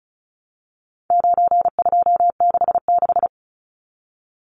Morse code, heard as a keyed tone, about 34 wpm, 700 Hz. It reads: 9266
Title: Numbers - 4-Digit 35wpm